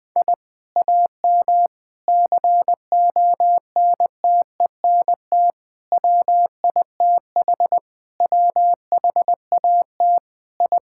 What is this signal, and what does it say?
Morse code, heard as a keyed tone, about 20 wpm, 700 Hz. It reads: I AM CONTENT WITH WHAT I
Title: Sentences from Top 1000 Words - Review Part 6 - 20wpm